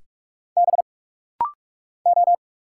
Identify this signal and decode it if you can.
Morse code, heard as a keyed tone, about 46 wpm, 690 Hz. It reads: B O
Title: Single Letters - Rapid-Fire 45wpm